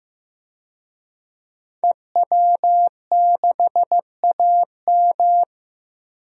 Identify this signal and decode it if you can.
Morse code, heard as a keyed tone, about 15 wpm, 700 Hz. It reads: EW6AM
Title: DXCC-100 Call Signs - 5 Characters - 15wpm